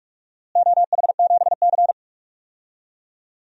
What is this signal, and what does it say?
Morse code, heard as a keyed tone, about 46 wpm, 700 Hz. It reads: OH7C
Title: DXCC-100 Call Signs - 4 Characters - Repeated 2x - 45wpm